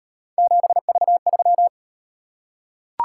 Morse code, heard as a keyed tone, about 34 wpm, 700 Hz. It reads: 7V3